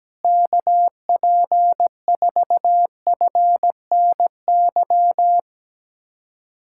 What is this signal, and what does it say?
Morse code, heard as a keyed tone, about 17 wpm, 700 Hz. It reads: KP4FNY
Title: USA Callsign DE Callsign 17wpm